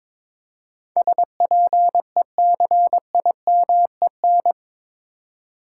Morse code, heard as a keyed tone, about 22 wpm, 700 Hz. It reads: SPECIMEN
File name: practice / 22wpm / Eight Letter Words - 22wpm